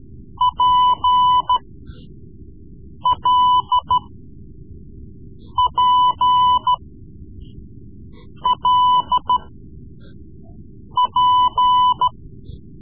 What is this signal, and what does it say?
Morse code, heard as a keyed tone, about 14 wpm, spaced as for 5 wpm, 1000 Hz. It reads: PLPLP
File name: Morse Lessons / Morse Lessons / Tape-3B